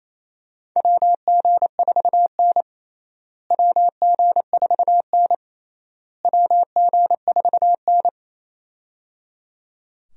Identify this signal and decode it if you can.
Morse code, heard as a keyed tone, about 28 wpm, 700 Hz. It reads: WG4N WG4N WG4N
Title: USA Call Signs - 4 Characters - Repeated 3x - 28wpm